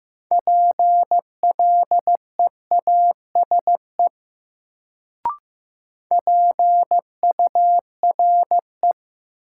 Morse code, heard as a keyed tone, about 15 wpm, 700 Hz. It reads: PLEASE PURE
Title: Sentences from Top 1500 Words - New Word Drill - 15wpm